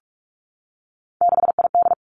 Morse code, heard as a keyed tone, about 46 wpm, 700 Hz. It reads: 6ID
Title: Three Challenging Letter-Number Combinations 45wpm